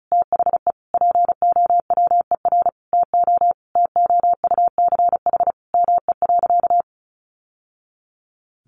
Morse code, heard as a keyed tone, about 34 wpm, 700 Hz. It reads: THE POWER TO TOUCH ME.
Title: Sentences from Top 1200 Words 35wpm